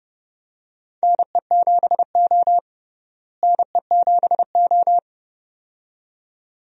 Morse code, heard as a keyed tone, about 30 wpm, 700 Hz. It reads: NE7O NE7O
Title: USA Call Signs - 4 Characters - Repeated 2x - 30wpm